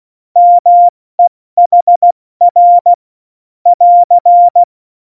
Morse code, heard as a keyed tone, about 16 wpm, 700 Hz. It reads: MEHR <AR>
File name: Morsekurs / Witze / Witz 19. Februar 2019, 70 F80